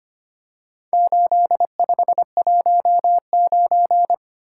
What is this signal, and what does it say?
Morse code, heard as a keyed tone, about 25 wpm, 700 Hz. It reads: 8519
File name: practice / 25wpm / Four Letter-Number Combinations - 25wpm